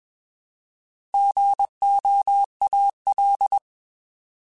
Morse code, heard as a keyed tone, about 21 wpm, 800 Hz. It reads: GOAL
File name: Alphabetical / Alphabetical mp3s / Gs @20